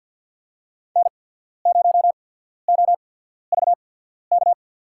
Morse code, heard as a keyed tone, about 49 wpm, spaced as for 14 wpm, 700 Hz. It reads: N0OVK
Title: USA Call Signs - 5 Characters - 8x Character Spacing - 50wpm